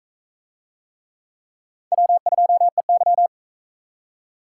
Morse code, heard as a keyed tone, about 40 wpm, 700 Hz. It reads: W2EY